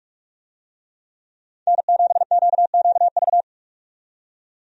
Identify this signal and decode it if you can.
Morse code, heard as a keyed tone, about 46 wpm, 700 Hz. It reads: N7QQV
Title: USA Call Signs - 5 Characters - 45wpm